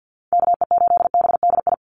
Morse code, heard as a keyed tone, about 50 wpm, 700 Hz. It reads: KE8BDI
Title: USA Call Signs 50wpm